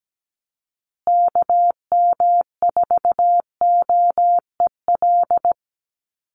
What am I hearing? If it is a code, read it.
Morse code, heard as a keyed tone, about 17 wpm, 700 Hz. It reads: KM4OEL
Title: USA Call Signs Repeated 2x 17wpm